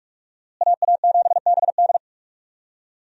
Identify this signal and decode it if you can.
Morse code, heard as a keyed tone, about 46 wpm, 700 Hz. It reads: AA7BD
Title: USA Call Signs - 5 Characters - Repeated 2x - 45wpm